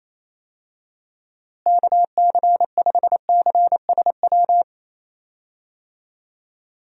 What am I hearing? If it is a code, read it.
Morse code, heard as a keyed tone, about 28 wpm, 700 Hz. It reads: KC5CSW
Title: USA Call Signs - 6 Characters - Repeated 3x - 28wpm